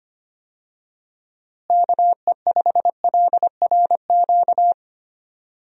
Morse code, heard as a keyed tone, about 25 wpm, 700 Hz. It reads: KE5LRQ
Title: USA Call Signs - 6 Characters - Repeated 2x - 25wpm